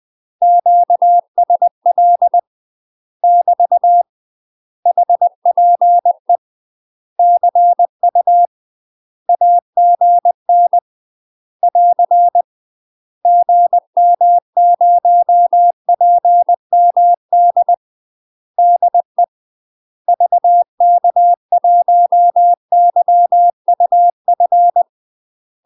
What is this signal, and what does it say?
Morse code, heard as a keyed tone, about 20 wpm, 700 Hz. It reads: QSL <BT> HPE CU AGN <AR> GM0PMD DE VK1YUF
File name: UK MASTER QSOs TRAINING / 20WPM / UK MASTER QSOs 0059